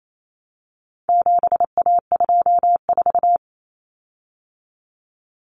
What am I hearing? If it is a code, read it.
Morse code, heard as a keyed tone, about 28 wpm, 700 Hz. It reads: 7A24
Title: Common Cut Numbers - 4-Digit 28wpm